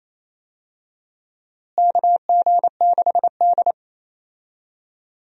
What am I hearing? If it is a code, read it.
Morse code, heard as a keyed tone, about 28 wpm, 700 Hz. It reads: KG6D
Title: USA Call Signs - 4 Characters - Repeated 2x - 28wpm